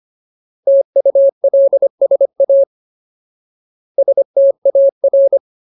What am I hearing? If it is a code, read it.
Morse code, heard as a keyed tone, about 25 wpm, 550 Hz. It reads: TULSA STAR
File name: Ultimate Word List / 25WPM Ultimate